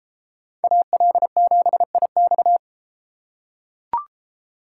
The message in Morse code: AL7IX

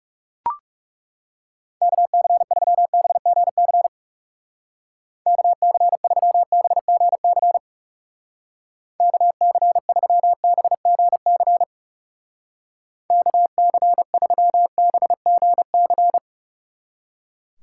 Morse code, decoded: KC3BGC KC3BGC KC3BGC KC3BGC

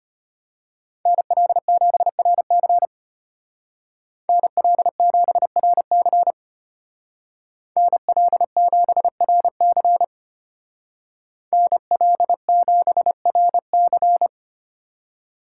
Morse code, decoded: NL7RC NL7RC NL7RC NL7RC

